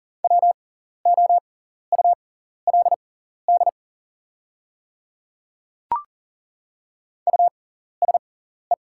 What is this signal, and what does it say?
Morse code, heard as a keyed tone, about 40 wpm, spaced as for 14 wpm, 700 Hz. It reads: WOULD USE
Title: 6x Character Spacing - Top 300 Words - 40wpm